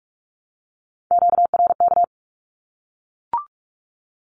Morse code, heard as a keyed tone, about 44 wpm, 700 Hz. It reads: QRK
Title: Q-Codes 45wpm